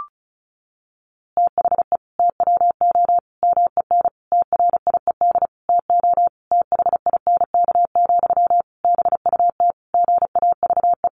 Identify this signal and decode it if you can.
Morse code, heard as a keyed tone, about 34 wpm, 700 Hz. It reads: THE TWO MEN TRIED TO THINK, BUT GAVE
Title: A Sound Of Thunder 35wpm